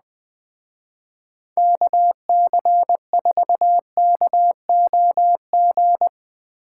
Morse code, heard as a keyed tone, about 20 wpm, 700 Hz. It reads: KC4KOG